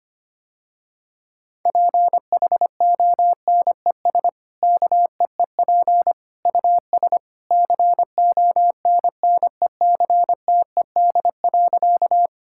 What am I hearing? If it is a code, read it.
Morse code, heard as a keyed tone, about 25 wpm, 690 Hz. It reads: PHONES KEEP US CONNECTED.
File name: Easy Sentences with Hints - 25wpm